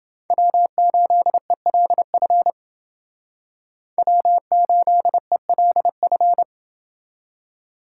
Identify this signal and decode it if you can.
Morse code, heard as a keyed tone, about 28 wpm, 700 Hz. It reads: W8ELF W8ELF